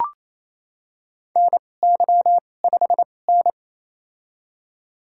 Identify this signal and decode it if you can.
Morse code, heard as a keyed tone, about 28 wpm, spaced as for 20 wpm, 700 Hz. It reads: NY5N
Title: USA Call Signs - 2x Character Spacing - 28wpm